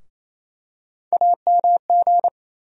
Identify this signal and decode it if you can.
Morse code, heard as a keyed tone, about 28 wpm, 700 Hz. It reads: AMG
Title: USA Call Sign Postfixes - 3 Letters 28wpm